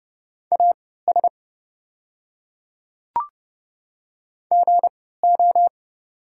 Morse code, heard as a keyed tone, about 30 wpm, spaced as for 17 wpm, 700 Hz. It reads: AS GO
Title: 3x Character Spacing - Top 100 Words - 30wpm